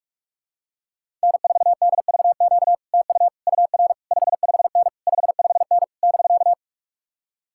Morse code, heard as a keyed tone, about 50 wpm, 700 Hz. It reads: N4DVQ TU UR 55N 55N <BK>